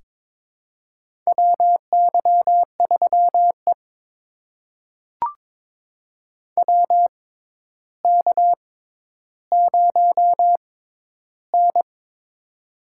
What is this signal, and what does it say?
Morse code, heard as a keyed tone, about 22 wpm, spaced as for 11 wpm, 700 Hz. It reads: WY3E WK0N
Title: USA Call Signs - 4 Characters - 6x Character Spacing - 22wpm